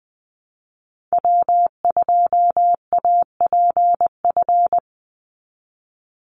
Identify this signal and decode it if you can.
Morse code, heard as a keyed tone, about 20 wpm, 700 Hz. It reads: W2APF
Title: Call Signs Repeated 3x - 20wpm